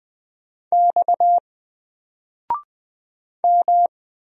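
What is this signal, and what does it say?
Morse code, heard as a keyed tone, about 20 wpm, 700 Hz. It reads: X M
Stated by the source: Single letter and number Morse drill at 20 wpm, from N4LCD.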